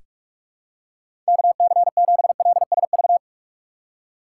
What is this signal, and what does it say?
Morse code, heard as a keyed tone, about 46 wpm, 700 Hz. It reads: KC7LSV
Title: USA Call Signs - 4x Character Spacing - 45wpm